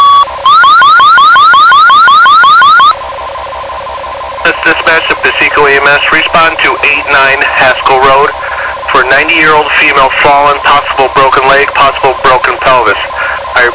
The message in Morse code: none

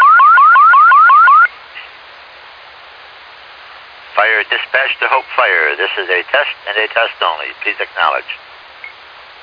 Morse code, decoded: none

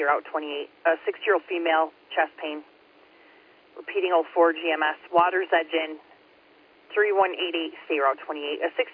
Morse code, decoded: none

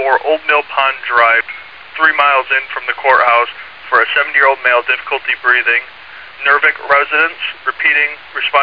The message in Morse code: none

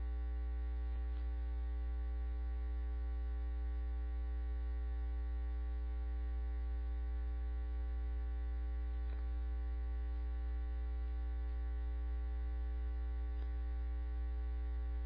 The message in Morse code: none

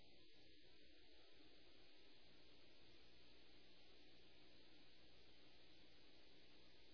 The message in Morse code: none

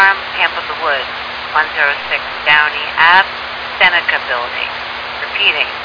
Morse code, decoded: none

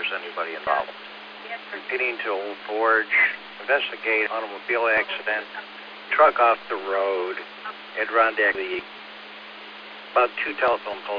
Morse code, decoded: none

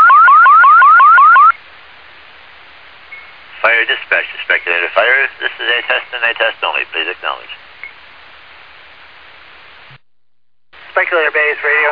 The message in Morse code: none

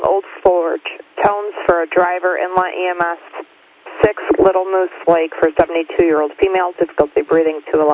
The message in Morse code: none

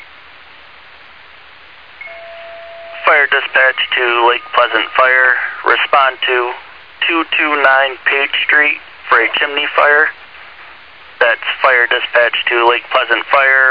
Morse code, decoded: none

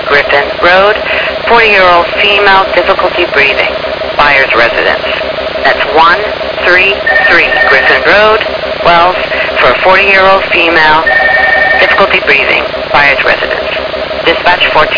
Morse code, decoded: none

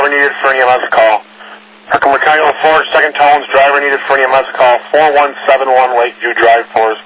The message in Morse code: none

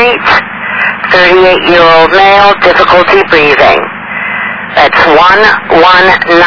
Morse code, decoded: none

